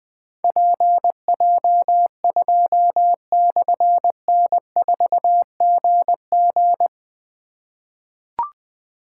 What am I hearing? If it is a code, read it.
Morse code, heard as a keyed tone, about 20 wpm, 700 Hz. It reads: PJ2/N4GG
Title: Difficult Call Signs 20wpm